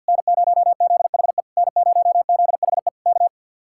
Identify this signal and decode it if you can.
Morse code, heard as a keyed tone, about 65 wpm, spaced as for 57 wpm, 700 Hz. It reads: N0ZHE N0ZHE K
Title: Speed-Racing Common QSO Phrases - 50wpm